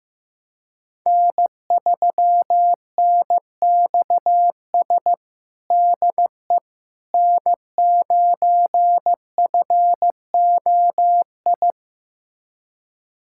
Morse code, heard as a keyed tone, about 16 wpm, 700 Hz. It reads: N3NXS DE N9FOI
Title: USA Callsign DE Callsign - 5 Characters - 15wpm